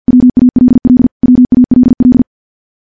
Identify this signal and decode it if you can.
Morse code, heard as a keyed tone, about 50 wpm, 250 Hz. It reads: WALL WALL
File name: morse / morse-training-50WPM